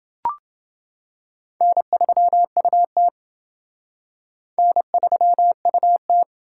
Morse code, heard as a keyed tone, about 28 wpm, 700 Hz. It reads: N3UT N3UT